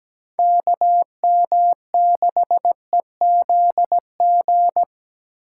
Morse code, heard as a keyed tone, about 20 wpm, 700 Hz. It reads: KM6EZG
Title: Speed Racing - USA 2x3 Call Signs 17wpm